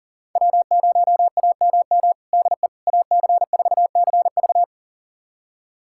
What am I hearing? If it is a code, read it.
Morse code, heard as a keyed tone, about 37 wpm, 700 Hz. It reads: W0AMM DE AC4CV
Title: USA Callsign DE Callsign 40wpm